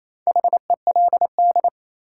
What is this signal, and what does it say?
Morse code, heard as a keyed tone, about 28 wpm, 700 Hz. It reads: HELD
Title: Sentences from Top 1600 Words - New Word Drill Alt - 28wpm